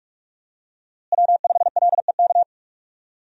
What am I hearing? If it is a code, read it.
Morse code, heard as a keyed tone, about 43 wpm, 700 Hz. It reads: W5LEK
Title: USA Call Signs - 5 Characters - Repeated 3x - 45wpm